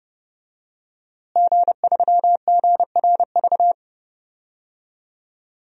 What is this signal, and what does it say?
Morse code, heard as a keyed tone, about 30 wpm, 700 Hz. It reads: G3GRV
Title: DXCC-100 Call Signs - 5 Characters - Repeated 3x - 30wpm